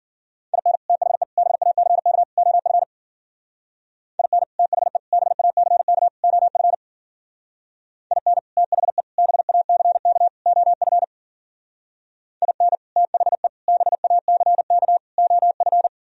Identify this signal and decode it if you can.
Morse code, heard as a keyed tone, about 50 wpm, 700 Hz. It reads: IN THE BACK OF IN THE BACK OF IN THE BACK OF IN THE BACK OF
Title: Speed-Racing - Sets of 4 Words 40wpm